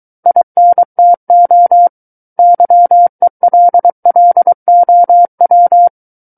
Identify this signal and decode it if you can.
Morse code, heard as a keyed tone, about 23 wpm, 700 Hz. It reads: INTO YELLOW
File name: Book One / 14-Chapter 14 - 23 WPM